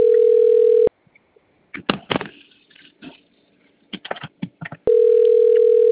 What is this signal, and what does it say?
Ringback tone (North America), 440 + 480 Hz.